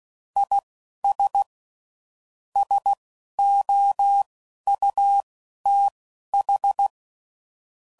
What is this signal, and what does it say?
Morse code, heard as a keyed tone, about 16 wpm, spaced as for 11 wpm, 800 Hz. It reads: IS SOUTH
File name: CW Code Groups @10WPM / QSOs / QSO kc7sac